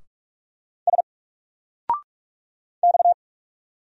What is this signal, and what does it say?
Morse code, heard as a keyed tone, about 46 wpm, 700 Hz. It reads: S X
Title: Single Letter-Number 45wpm